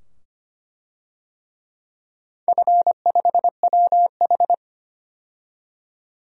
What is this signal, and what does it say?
Morse code, heard as a keyed tone, about 25 wpm, 700 Hz. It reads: F5WH
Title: DXCC-100 Call Signs - 4 Characters - 25wpm